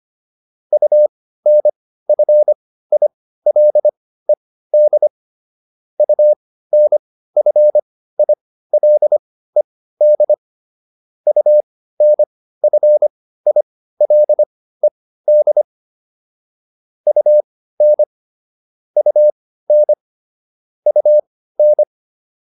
Morse code, heard as a keyed tone, about 25 wpm, spaced as for 15 wpm, 600 Hz. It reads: UNFILED UNFILED UNFILED UN UN UN